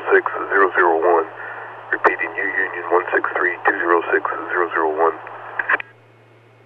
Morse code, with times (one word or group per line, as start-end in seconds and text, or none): none